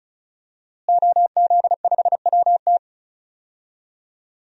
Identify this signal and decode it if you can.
Morse code, heard as a keyed tone, about 34 wpm, 700 Hz. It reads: OZ5WT
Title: DXCC-100 Call Signs - 5 Characters - Repeated 3x - 35wpm